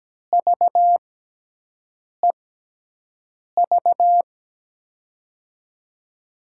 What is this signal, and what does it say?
Morse code, heard as a keyed tone, about 17 wpm, spaced as for 6 wpm, 700 Hz. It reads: VEV
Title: USA Call Signs - 6 Characters - 6x Character Spacing - 17wpm